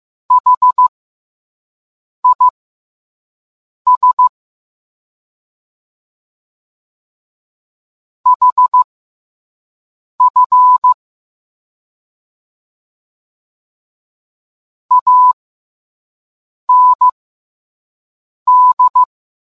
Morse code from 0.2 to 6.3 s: HIS